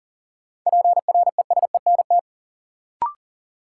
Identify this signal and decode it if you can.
Morse code, heard as a keyed tone, about 40 wpm, 700 Hz. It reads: PRESENT
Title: Sentences from Top 800 Words - New Word Drill - 40wpm